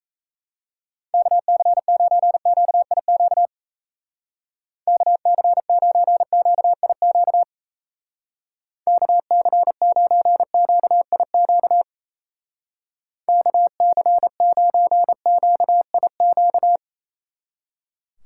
Morse code, heard as a keyed tone, about 34 wpm, 700 Hz. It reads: KC9QIQ KC9QIQ KC9QIQ KC9QIQ
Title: Speed Racing - USA 2x3 Call Signs 28wpm